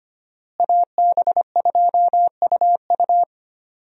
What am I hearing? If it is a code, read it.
Morse code, heard as a keyed tone, about 25 wpm, 700 Hz. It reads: AB2UU
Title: USA Call Signs - 5 Characters - Repeated 3x - 25wpm